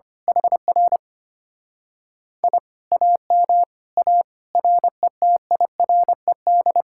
Morse code, heard as a keyed tone, about 24 wpm, 700 Hz. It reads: HR I AM A RETIRED
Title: Speed-Racing Common QSO Phrases Encore - 25wpm